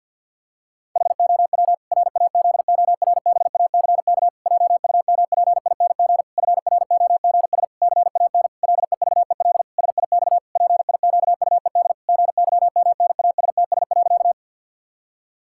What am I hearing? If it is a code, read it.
Morse code, heard as a keyed tone, about 52 wpm, 700 Hz. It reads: HOW RAZORBACK JUMPING FROGS CAN LEVEL SIX PIQUED GYMNASTS.
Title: Pangram Challenge v2 - 50wpm